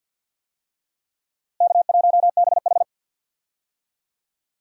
Morse code, heard as a keyed tone, about 51 wpm, 700 Hz. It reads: K1BH
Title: USA Call Signs - 4 Characters - Repeated 3x - 50wpm